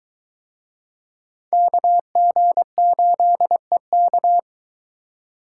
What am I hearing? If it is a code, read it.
Morse code, heard as a keyed tone, about 22 wpm, 700 Hz. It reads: KG8EK